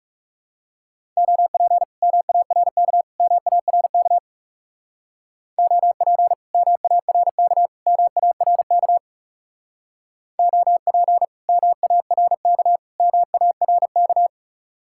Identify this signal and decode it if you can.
Morse code, heard as a keyed tone, about 36 wpm, 700 Hz. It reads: OP MARK MARK OP MARK MARK OP MARK MARK
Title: Speed-Racing Common QSO Phrases Encore - 30wpm